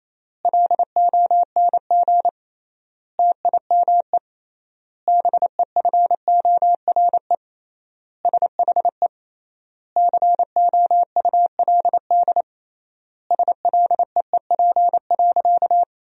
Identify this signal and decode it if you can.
Morse code, heard as a keyed tone, about 27 wpm, 700 Hz. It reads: LONG TIME BEFORE SHE COULD SLEEP.
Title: 3x Word-Spacing - Sentences from Top 500 Words - 28wpm